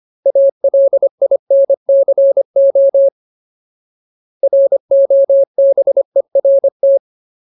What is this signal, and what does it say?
Morse code, heard as a keyed tone, about 25 wpm, 550 Hz. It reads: ALINCO ROBERT
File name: Ultimate Word List / 25WPM Ultimate